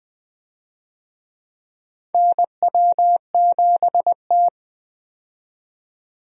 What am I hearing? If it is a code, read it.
Morse code, heard as a keyed tone, about 20 wpm, 700 Hz. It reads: NW7T
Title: USA Call Signs - 4 Characters - 20wpm